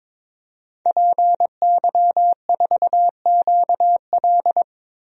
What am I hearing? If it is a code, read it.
Morse code, heard as a keyed tone, about 22 wpm, 700 Hz. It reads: PY4QL
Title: DXCC-100 Call Signs - 5 Characters - Repeated 2x - 22wpm